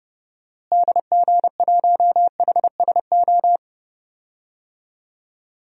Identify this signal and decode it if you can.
Morse code, heard as a keyed tone, about 30 wpm, 700 Hz. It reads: DG1HSO
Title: DXCC-100 Call Signs - 6 Characters - Repeated 2x - 30wpm